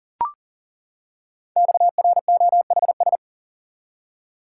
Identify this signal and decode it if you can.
Morse code, heard as a keyed tone, about 41 wpm, 700 Hz. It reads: XROHS
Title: Code Groups - 5 Letters - 40wpm